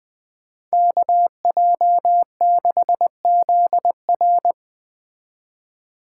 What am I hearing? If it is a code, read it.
Morse code, heard as a keyed tone, about 20 wpm, 700 Hz. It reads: KJ6ZR
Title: Speed Racing - USA Call Signs 17wpm